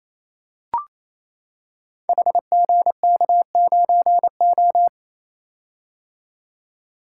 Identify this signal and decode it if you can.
Morse code, heard as a keyed tone, about 27 wpm, 700 Hz. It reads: HGK9O
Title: Code Groups - 5 Letter:Numbers - 28wpm